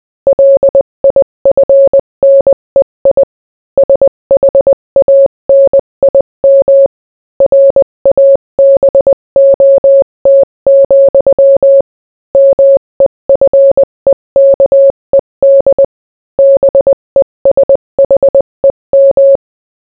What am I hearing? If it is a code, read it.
Morse code, heard as a keyed tone, about 20 wpm, 550 Hz. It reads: LIFNEI SHANIM RABOT, MEFEKED BESHEM